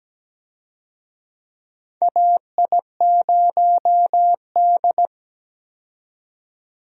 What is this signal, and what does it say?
Morse code, heard as a keyed tone, about 17 wpm, 700 Hz. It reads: AI0D
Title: USA Call Signs - 4 Characters - 17wpm